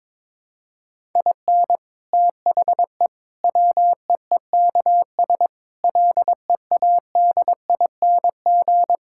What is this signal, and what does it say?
Morse code, heard as a keyed tone, about 22 wpm, 700 Hz. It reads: IN THE WEEKS LEADING
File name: practice / 22wpm / Sets of 7 Words - 22wpm